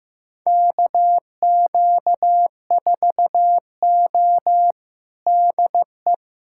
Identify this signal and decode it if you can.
Morse code, heard as a keyed tone, about 15 wpm, 700 Hz. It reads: KQ4O DE
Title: USA Callsign DE Callsign - 4 Characters - 15wpm